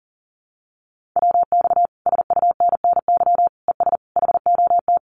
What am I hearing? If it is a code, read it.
Morse code, heard as a keyed tone, about 42 wpm, 700 Hz. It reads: WX SUNNY ES HOT
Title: Common QSO Phrases Encore 40wpm